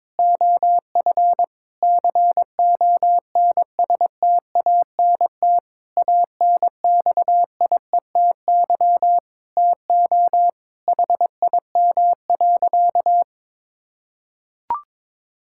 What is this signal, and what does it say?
Morse code, heard as a keyed tone, about 22 wpm, 700 Hz. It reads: OF CONSTANT ANXIETY TO HIM.
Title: Sentences from Top 1600 Words 22wpm